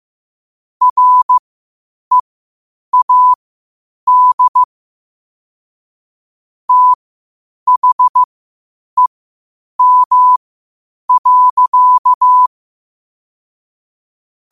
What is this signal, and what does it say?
Morse code, heard as a keyed tone, about 15 wpm, spaced as for 8 wpm, 1000 Hz. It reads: READ THEM.
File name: qotdfiles / 8wpm